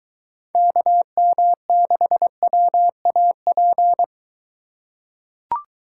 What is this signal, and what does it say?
Morse code, heard as a keyed tone, about 22 wpm, 700 Hz. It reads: KM6WAP